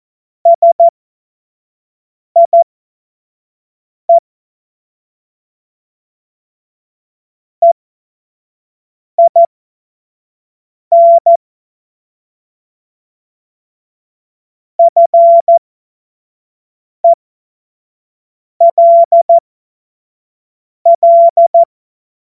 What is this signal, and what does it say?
Morse code, heard as a keyed tone, about 14 wpm, spaced as for 5 wpm, 680 Hz. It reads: SIE EIN FELL